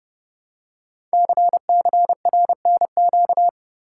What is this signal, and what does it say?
Morse code, heard as a keyed tone, about 30 wpm, 700 Hz. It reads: CCRNQ
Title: Code Groups - 5 Letter:Numbers - 30wpm